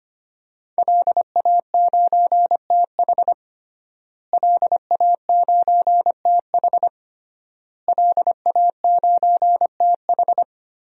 Morse code, heard as a keyed tone, about 25 wpm, 700 Hz. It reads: LA9TH LA9TH LA9TH
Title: DXCC-100 Call Signs - 5 Characters - Repeated 3x - 25wpm